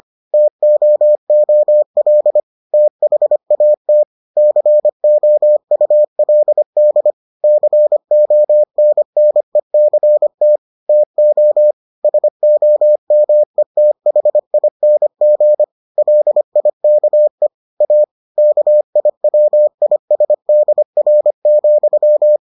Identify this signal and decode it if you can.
Morse code, heard as a keyed tone, about 25 wpm, 600 Hz. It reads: TOOL THAT COULD CONNECT TO SOMETHING LIKE A KIWISDR,